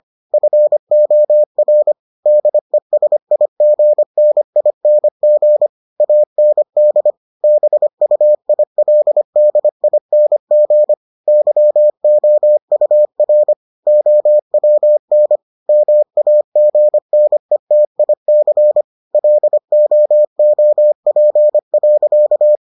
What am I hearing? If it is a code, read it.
Morse code, heard as a keyed tone, about 25 wpm, 600 Hz. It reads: FOR DESIGNING AND BUILDING YOUR OWN MAGNETIC LOOP.